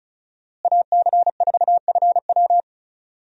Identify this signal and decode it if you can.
Morse code, heard as a keyed tone, about 36 wpm, 700 Hz. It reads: AC4FW
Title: USA Call Signs - 5 Characters - Repeated 3x - 35wpm